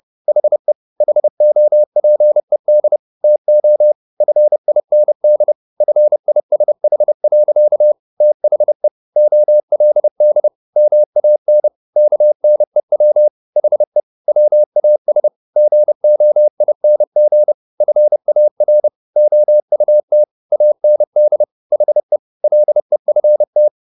Morse code, heard as a keyed tone, about 30 wpm, 600 Hz. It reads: HE HOPED TO FIND FISH. THE OLD MAN KNEW HE WAS GOING FAR OUT AND HE LEFT
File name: the old man and the sea / the old man and the sea 1 30wpm